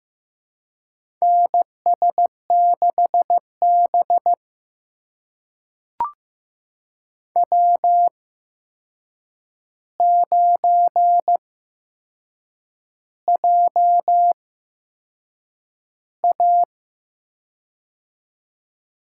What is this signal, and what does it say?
Morse code, heard as a keyed tone, about 15 wpm, spaced as for 6 wpm, 700 Hz. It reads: NS6B W9JA
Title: USA Call Signs - 4 Characters - 8x Character Spacing - 15wpm